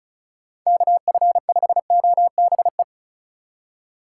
Morse code, heard as a keyed tone, about 34 wpm, 700 Hz. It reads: KF5OBE
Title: USA Call Signs - 6 Characters - 8x Character Spacing - 35wpm